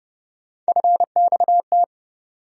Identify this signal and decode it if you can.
Morse code, heard as a keyed tone, about 32 wpm, 700 Hz. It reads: FXT
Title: USA Call Sign Postfixes - 3 Letters 30wpm